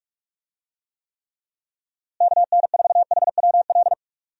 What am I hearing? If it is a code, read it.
Morse code, heard as a keyed tone, about 46 wpm, 700 Hz. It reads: KN4HWL
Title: USA Call Signs - 6 Characters - Repeated 3x - 45wpm